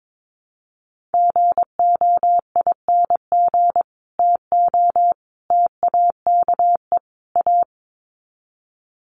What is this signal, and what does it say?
Morse code, heard as a keyed tone, about 22 wpm, 700 Hz. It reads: GOING TO TAKE A
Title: Sets of 4 Words 22wpm